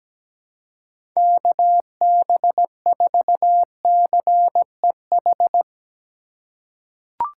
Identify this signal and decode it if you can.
Morse code, heard as a keyed tone, about 17 wpm, 700 Hz. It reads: KB4CEH
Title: USA Call Signs - 6 Characters - 4x Character Spacing - 17wpm